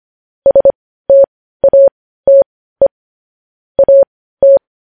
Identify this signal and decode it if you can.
Morse code, heard as a keyed tone, about 25 wpm, spaced as for 15 wpm, 550 Hz. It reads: STATE AT
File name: podcast / News headlines-1x-25x15